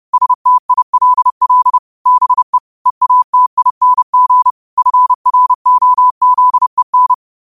Morse code, heard as a keyed tone, about 30 wpm, 1000 Hz. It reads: STILL BE EATING FROZEN
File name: 30wpm